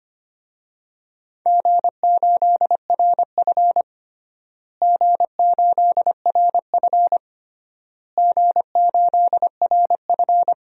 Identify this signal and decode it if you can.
Morse code, heard as a keyed tone, about 25 wpm, 700 Hz. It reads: G8RF G8RF G8RF
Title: DXCC-100 Call Signs - 4 Characters - Repeated 3x - 25wpm